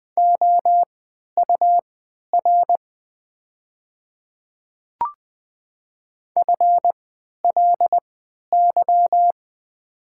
Morse code, heard as a keyed tone, about 20 wpm, spaced as for 11 wpm, 700 Hz. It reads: OUR FLY
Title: 3x Character Spacing - Top 500 Words - 20wpm